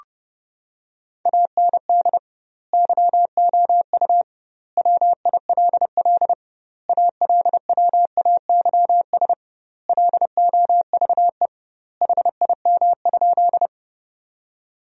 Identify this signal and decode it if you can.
Morse code, heard as a keyed tone, about 30 wpm, 700 Hz. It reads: AND YOU WILL ALWAYS LOVE HIM?
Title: 2x Word-Spacing - Sentences from Top 400 Words - 30wpm